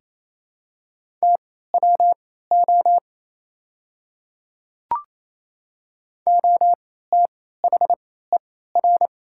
Morse code, heard as a keyed tone, about 28 wpm, spaced as for 16 wpm, 690 Hz. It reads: TWO OTHER
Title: 3x Character Spacing - Top 100 Words - 28wpm